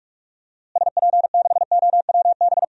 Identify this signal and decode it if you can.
Morse code, heard as a keyed tone, about 42 wpm, 700 Hz. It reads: SP6OWB
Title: DXCC-100 Call Signs - 6 Characters - Repeated 2x - 45wpm